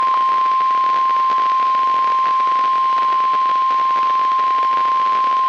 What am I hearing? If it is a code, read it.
Emergency Alert System (SAME) tones (the 1050 Hz attention tone), with noise. It is heard from 0.8 to 5.1 s.
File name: eas messages / EAS WXL38 Binghamton